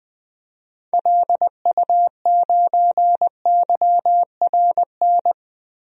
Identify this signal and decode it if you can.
Morse code, heard as a keyed tone, about 20 wpm, 700 Hz. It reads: LU9YRN
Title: DXCC-100 Call Signs - 6 Characters - 20wpm